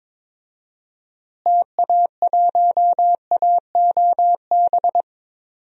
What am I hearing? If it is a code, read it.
Morse code, heard as a keyed tone, about 22 wpm, 700 Hz. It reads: TA1AOB
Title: DXCC-100 Call Signs - 6 Characters - Repeated 2x - 22wpm